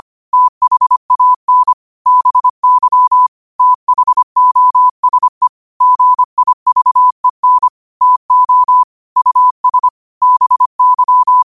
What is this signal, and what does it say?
Morse code, heard as a keyed tone, about 25 wpm, 1000 Hz. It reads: THAN BY THOSE GIVEN TO US BY
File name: 25wpm